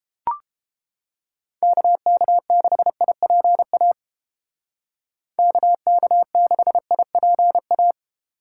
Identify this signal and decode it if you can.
Morse code, heard as a keyed tone, about 30 wpm, 700 Hz. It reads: KK6IPA KK6IPA